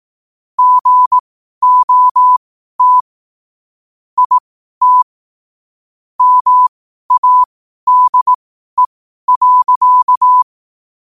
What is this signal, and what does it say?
Morse code, heard as a keyed tone, about 18 wpm, spaced as for 12 wpm, 1000 Hz. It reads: GOT IT MADE.
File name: qotdfiles / 12wpm